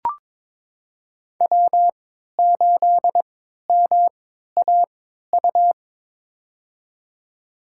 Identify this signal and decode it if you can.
Morse code, heard as a keyed tone, about 23 wpm, spaced as for 13 wpm, 700 Hz. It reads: W8MAU
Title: USA Call Signs - 3x Character Spacing - 22wpm